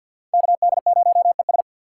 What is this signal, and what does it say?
Morse code, heard as a keyed tone, about 53 wpm, 700 Hz. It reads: KD0ES
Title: USA Call Signs - 5 Characters - 4x Character Spacing - 50wpm